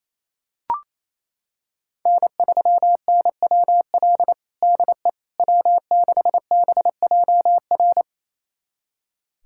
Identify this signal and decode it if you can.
Morse code, heard as a keyed tone, about 27 wpm, 700 Hz. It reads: N3NWL DE W6BJR
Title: USA Callsign DE Callsign - 5 Characters - 28wpm